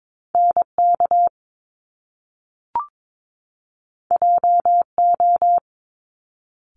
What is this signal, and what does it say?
Morse code, heard as a keyed tone, about 22 wpm, 700 Hz. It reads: NK JO
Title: Top Two Letter Combinations in Words 22wpm